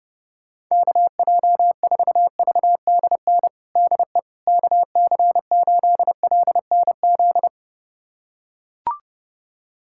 Morse code, heard as a keyed tone, about 32 wpm, 700 Hz. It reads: KJ4VDN DE KC8LNZ
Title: USA Callsign DE Callsign - 6 Characters - 30wpm